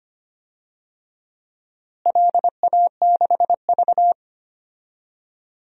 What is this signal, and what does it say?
Morse code, heard as a keyed tone, about 25 wpm, 700 Hz. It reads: LA6V
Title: DXCC-100 Call Signs - 4 Characters - Repeated 2x - 25wpm